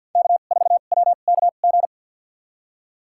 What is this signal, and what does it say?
Morse code, heard as a keyed tone, about 47 wpm, spaced as for 35 wpm, 700 Hz. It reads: K4WKG
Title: USA Call Signs - 5 Characters - 2x Character Spacing - 50wpm